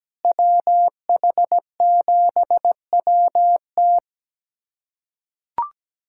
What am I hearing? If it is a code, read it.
Morse code, heard as a keyed tone, about 17 wpm, 700 Hz. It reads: WH7WT
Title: USA Callsign DE Callsign - 5 Characters - 17wpm